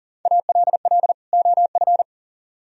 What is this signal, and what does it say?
Morse code, heard as a keyed tone, about 48 wpm, 700 Hz. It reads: ALL OF
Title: Speed-Racing - Sets of 2 Words 40wpm